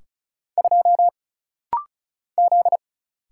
Morse code, heard as a keyed tone, about 36 wpm, 700 Hz. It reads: 2 Z
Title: Single Letter-Number Alt - Rapid-Fire 35wpm